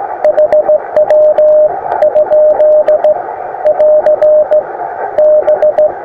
Morse code, heard as a keyed tone, about 15 wpm, 610 Hz, with radio noise. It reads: HW? <AR> B
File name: attachments / iz0aym 20m